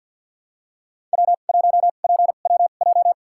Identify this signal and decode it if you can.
Morse code, heard as a keyed tone, about 52 wpm, spaced as for 36 wpm, 700 Hz. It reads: W1PWJ
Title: USA Call Signs - 5 Characters - 2x Character Spacing - 50wpm